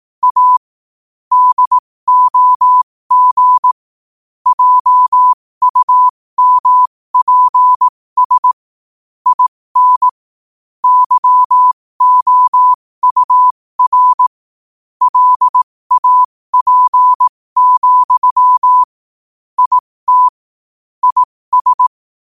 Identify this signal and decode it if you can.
Morse code, heard as a keyed tone, about 18 wpm, spaced as for 15 wpm, 1000 Hz. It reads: A DOG JUMPS IN YOUR LAP, IT IS
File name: qotdfiles / 15wpm